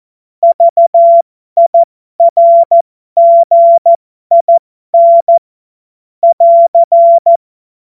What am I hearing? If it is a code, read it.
Morse code, heard as a keyed tone, about 14 wpm, spaced as for 12 wpm, 680 Hz. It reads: VIRGIN <AR>
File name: Morsekurs / Witze